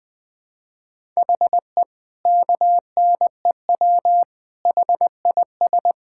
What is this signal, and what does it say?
Morse code, heard as a keyed tone, about 20 wpm, 700 Hz. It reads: HE KNEW HIS
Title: All-Age Appropriate Jokes - 20wpm